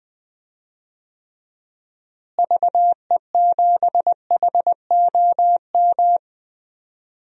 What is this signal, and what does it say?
Morse code, heard as a keyed tone, about 21 wpm, 700 Hz. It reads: VE7HOM
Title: DXCC-100 Call Signs - 6 Characters - Repeated 3x - 20wpm